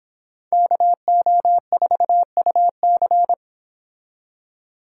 Morse code, heard as a keyed tone, about 25 wpm, 700 Hz. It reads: KO4UC